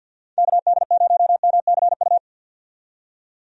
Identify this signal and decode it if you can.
Morse code, heard as a keyed tone, about 50 wpm, 700 Hz. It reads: KD0MCU
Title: USA Call Signs - 6 Characters - 50wpm